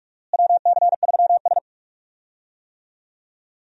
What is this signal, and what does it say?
Morse code, heard as a keyed tone, about 46 wpm, 700 Hz. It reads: WC3S
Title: USA Call Signs - 4 Characters - Repeated 3x - 45wpm